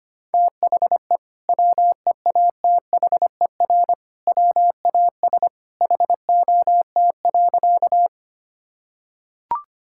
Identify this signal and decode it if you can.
Morse code, heard as a keyed tone, about 25 wpm, 700 Hz. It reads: THE WEATHER WAS HOT.
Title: Sentences from Top 1000 Words - Review Part 5 - 25wpm